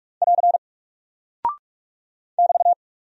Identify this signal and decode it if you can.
Morse code, heard as a keyed tone, about 44 wpm, 700 Hz. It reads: <AR> <BT>
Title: Single Character (Letter:Number:Punctuation:Prosign) 45wpm